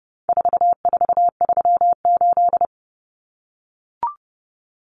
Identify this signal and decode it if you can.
Morse code, heard as a keyed tone, about 30 wpm, 700 Hz. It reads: 4438
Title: Numbers - 4-Digit 30wpm